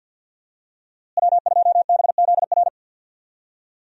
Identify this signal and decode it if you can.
Morse code, heard as a keyed tone, about 52 wpm, 700 Hz. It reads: W2BZR
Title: USA Call Signs - 5 Characters - Repeated 3x - 50wpm